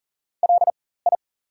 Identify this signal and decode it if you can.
Morse code, heard as a keyed tone, about 40 wpm, spaced as for 19 wpm, 700 Hz. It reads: LI
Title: 4x Character Spacing - Top 300 Words - 40wpm